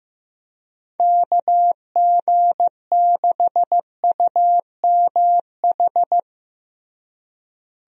Morse code, heard as a keyed tone, about 15 wpm, 700 Hz. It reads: KG6UMH
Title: USA Call Signs - 6 Characters - Repeated 2x - 15wpm